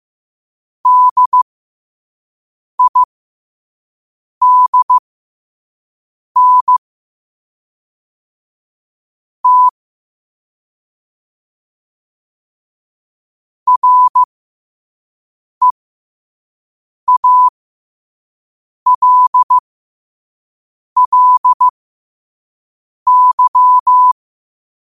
Morse code, heard as a keyed tone, about 15 wpm, spaced as for 5 wpm, 1000 Hz. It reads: DIDN T REALLY
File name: qotdfiles / 5wpm